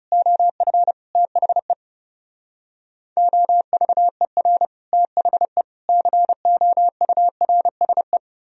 Morse code, heard as a keyed tone, about 40 wpm, 700 Hz. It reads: OF THE OVER THE COURSE